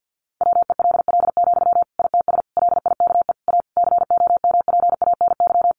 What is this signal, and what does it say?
Morse code, heard as a keyed tone, about 50 wpm, 700 Hz. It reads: WELL, ITS LIKE A COMPANY
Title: Sentences from Top 600 Words 50wpm